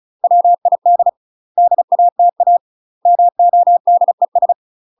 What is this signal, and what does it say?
Morse code, heard as a keyed tone, about 36 wpm, 700 Hz. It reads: WID DATA MODES
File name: UK MASTER QSOs TRAINING / 35WPM / UK MASTER QSOs 0018